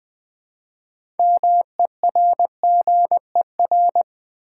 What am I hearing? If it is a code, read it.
Morse code, heard as a keyed tone, about 20 wpm, 700 Hz. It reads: MERGER
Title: Build A Word - 3 Letters At A Time - 6-Letter Words - 20wpm